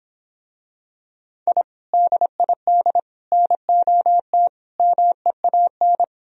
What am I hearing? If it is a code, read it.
Morse code, heard as a keyed tone, about 26 wpm, 700 Hz. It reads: I DID NOT MEAN